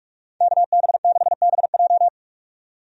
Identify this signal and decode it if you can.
Morse code, heard as a keyed tone, about 46 wpm, 700 Hz. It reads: KB6BJ
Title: USA Call Signs - 5 Characters - 45wpm